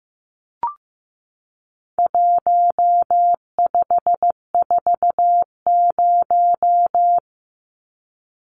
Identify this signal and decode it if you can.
Morse code, heard as a keyed tone, about 15 wpm, 700 Hz. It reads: 1540
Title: Numbers - 4-Digit 15wpm